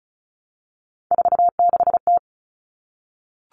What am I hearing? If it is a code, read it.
Morse code, heard as a keyed tone, about 34 wpm, 700 Hz. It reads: 46T
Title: Common Cut Numbers - 3-Digit 35wpm